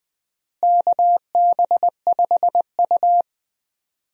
Morse code, heard as a keyed tone, about 21 wpm, 700 Hz. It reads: KB5U
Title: USA Call Signs - 4 Characters - Repeated 3x - 20wpm